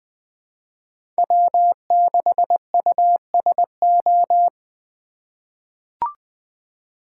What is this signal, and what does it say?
Morse code, heard as a keyed tone, about 21 wpm, 700 Hz. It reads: W6USO
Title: USA Call Signs - 5 Characters - 3x Character Spacing - 20wpm